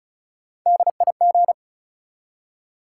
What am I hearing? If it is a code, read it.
Morse code, heard as a keyed tone, about 36 wpm, 700 Hz. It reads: DIG
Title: Three Letter Words - 35wpm